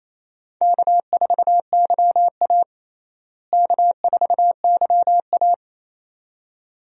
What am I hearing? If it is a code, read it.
Morse code, heard as a keyed tone, about 28 wpm, 700 Hz. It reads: K4YA K4YA
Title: USA Call Signs - 4 Characters - Repeated 2x - 28wpm